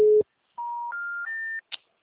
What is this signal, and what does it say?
Special information tone.